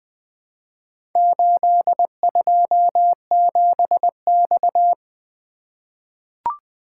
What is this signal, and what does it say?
Morse code, heard as a keyed tone, about 20 wpm, 700 Hz. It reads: 827X
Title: Four Letter-Number Combinations - 20wpm